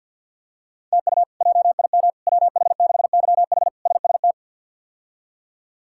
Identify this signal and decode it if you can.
Morse code, heard as a keyed tone, about 51 wpm, 700 Hz. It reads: TU JIM WH6YH SST
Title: K1USN SST QSO Practice - 50wpm